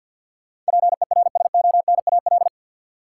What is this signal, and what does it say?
Morse code, heard as a keyed tone, about 52 wpm, 700 Hz. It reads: PERSONAL
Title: Sentences from Top 700 Words - New Word Drill - 50wpm